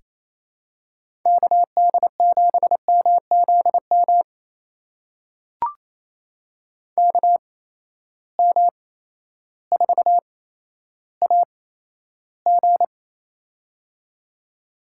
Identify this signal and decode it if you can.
Morse code, heard as a keyed tone, about 28 wpm, spaced as for 13 wpm, 700 Hz. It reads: KD7MZM KM4AG
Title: USA Call Signs - 8x Character Spacing - 28wpm